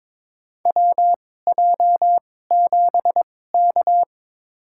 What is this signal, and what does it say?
Morse code, heard as a keyed tone, about 22 wpm, spaced as for 16 wpm, 700 Hz. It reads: WJ7K
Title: USA Call Signs - 2x Character Spacing - 22wpm